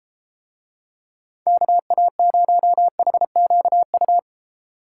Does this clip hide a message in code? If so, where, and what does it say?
Yes: Morse code from 1.5 to 4.2 s, heard as a keyed tone, about 33 wpm, 700 Hz. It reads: KA0HQU